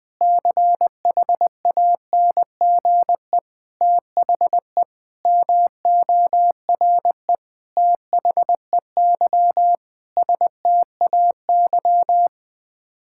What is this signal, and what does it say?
Morse code, heard as a keyed tone, about 20 wpm, 700 Hz. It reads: CHANGE THE MORE THEY STAY
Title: Sets of 8 Words - 20wpm